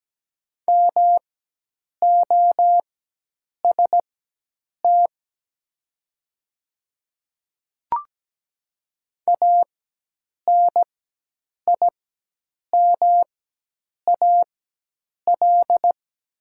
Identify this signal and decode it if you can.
Morse code, heard as a keyed tone, about 17 wpm, spaced as for 8 wpm, 700 Hz. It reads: MOST ANIMAL
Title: 4x Character Spacing - Top 500 Words - 17wpm